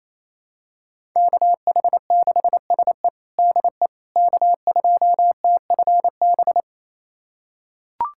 Morse code, heard as a keyed tone, about 28 wpm, 700 Hz. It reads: KH6SE DE K2TFB
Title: USA Callsign DE Callsign - 5 Characters - 28wpm